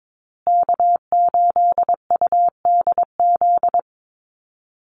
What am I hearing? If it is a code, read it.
Morse code, heard as a keyed tone, about 22 wpm, 700 Hz. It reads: K8UDZ
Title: USA Call Signs Repeated 3x 22wpm